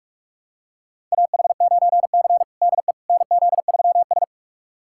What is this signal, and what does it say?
Morse code, heard as a keyed tone, about 45 wpm, 700 Hz. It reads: AH9C DE NZ3S
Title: USA Callsign DE Callsign - 4 Characters - 45wpm